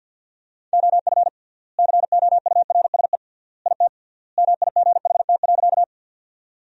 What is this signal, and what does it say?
Morse code, heard as a keyed tone, about 50 wpm, 700 Hz. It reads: OF COURSE IT MIGHT.
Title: 3x Word-Spacing - Sentences from Top 500 Words - 50wpm